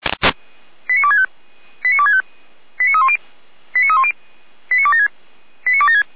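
Five-tone selective call (ZVEI), heard as quick runs of five tones, tones of about 70 ms. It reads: 89265 89265 89310 89310 89276 89276
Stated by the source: ZVEI five-tone selective call received on an SDR, channel 455, by Feuerwehr Orlamünde.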